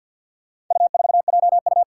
Morse code, heard as a keyed tone, about 49 wpm, 700 Hz. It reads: U4JU
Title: Four Challenging Letter-Number Combinations 50wpm